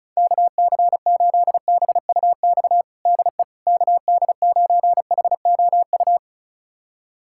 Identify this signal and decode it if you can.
Morse code, heard as a keyed tone, about 36 wpm, 700 Hz. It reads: KC8BUX DE KD9HOU
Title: USA Callsign DE Callsign - 6 Characters - 35wpm